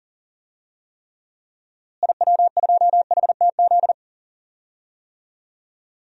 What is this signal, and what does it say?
Morse code, heard as a keyed tone, about 40 wpm, 700 Hz. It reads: IW2HTZ
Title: DXCC-100 Call Signs - 6 Characters - Repeated 3x - 40wpm